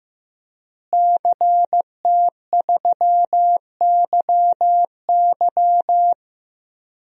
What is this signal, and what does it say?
Morse code, heard as a keyed tone, about 15 wpm, 700 Hz. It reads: CT3YY
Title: DXCC-100 Call Signs - 5 Characters - Repeated 3x - 15wpm